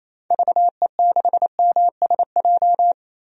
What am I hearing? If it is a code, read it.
Morse code, heard as a keyed tone, about 28 wpm, 700 Hz. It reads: VE6MSJ
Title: DXCC-100 Call Signs - 6 Characters - Repeated 3x - 28wpm